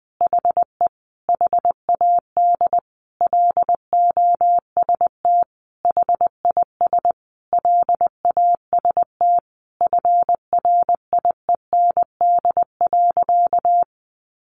Morse code, heard as a keyed tone, about 19 wpm, 700 Hz. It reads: HE HAD LOST HIS LAST FRIEND.
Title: Sentences from Top 1400 Words 20wpm